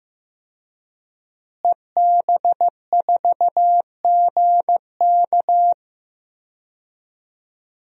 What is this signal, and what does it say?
Morse code, heard as a keyed tone, about 15 wpm, 700 Hz. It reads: EB4GK
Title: DXCC-100 Call Signs - 5 Characters - Repeated 2x - 15wpm